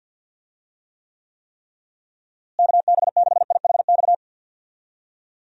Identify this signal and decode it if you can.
Morse code, heard as a keyed tone, about 51 wpm, 700 Hz. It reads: KB6IHX
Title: USA Call Signs - 6 Characters - Repeated 3x - 50wpm